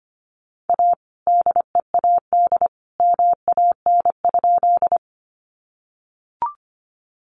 Morse code, heard as a keyed tone, about 25 wpm, 700 Hz. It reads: A DEAD MAN?
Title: Sentences from Top 1500 Words 25wpm